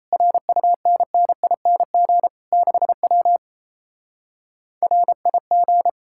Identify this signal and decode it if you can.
Morse code, heard as a keyed tone, about 38 wpm, 700 Hz. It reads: RUNNING 6W RIG